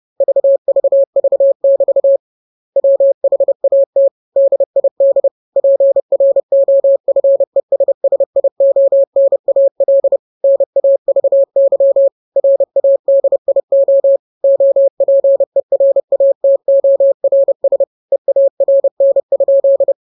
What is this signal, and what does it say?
Morse code, heard as a keyed tone, about 30 wpm, 550 Hz. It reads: VVV<BT> WHAT DID PROFESSIONAL NAVY RADIO OPERATORS EARN?